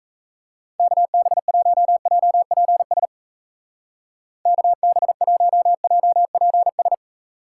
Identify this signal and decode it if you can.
Morse code, heard as a keyed tone, about 35 wpm, 700 Hz. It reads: KB1JPS KB1JPS
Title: Speed Racing - USA 2x3 Call Signs 28wpm